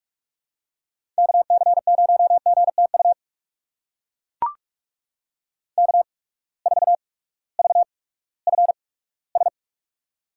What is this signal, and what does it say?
Morse code, heard as a keyed tone, about 44 wpm, spaced as for 20 wpm, 690 Hz. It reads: KC0GTU K4VFS
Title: USA Call Signs - 8x Character Spacing - 45wpm